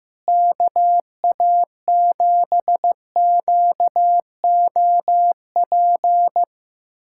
Morse code, heard as a keyed tone, about 15 wpm, 700 Hz. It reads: KA7QOP
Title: USA Callsign DE Callsign 15wpm